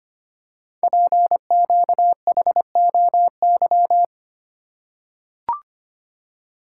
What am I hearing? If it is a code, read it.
Morse code, heard as a keyed tone, about 25 wpm, 700 Hz. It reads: PQHOY
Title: Code Groups - 5 Letters - 25wpm